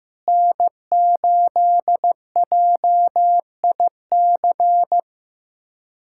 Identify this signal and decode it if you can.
Morse code, heard as a keyed tone, about 15 wpm, 700 Hz. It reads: N8JIC
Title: USA Callsign DE Callsign - 5 Characters - 15wpm